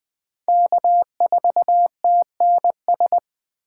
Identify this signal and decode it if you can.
Morse code, heard as a keyed tone, about 20 wpm, 700 Hz. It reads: K4TNS